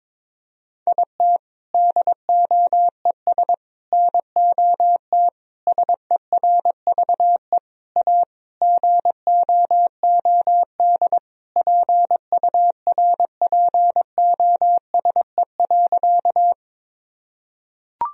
Morse code, heard as a keyed tone, about 22 wpm, 700 Hz. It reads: IT DOES NOT SERVE A GOOD PURPOSE.
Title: Sentences from Top 1000 Words - Review Part 2 - 22wpm